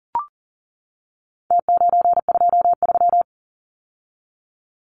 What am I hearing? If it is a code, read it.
Morse code, heard as a keyed tone, about 40 wpm, 700 Hz. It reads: T923
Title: Numbers - Contest Style - 4-Digit 40wpm